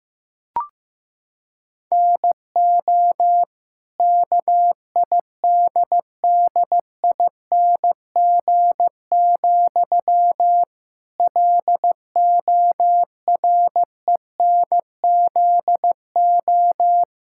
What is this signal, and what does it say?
Morse code, heard as a keyed tone, about 15 wpm, 700 Hz. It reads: NO KIDDING, LORENZO
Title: Pangram Challenge v2 - 15wpm